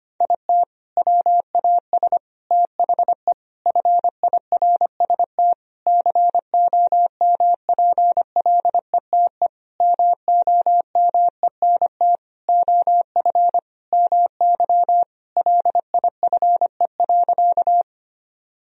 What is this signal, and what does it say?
Morse code, heard as a keyed tone, about 25 wpm, 700 Hz. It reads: IT WAS THE FIRST COMPLETE MOMENT OF MY LIFE.
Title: Sentences from Top 1000 Words - Review Part 2 - 25wpm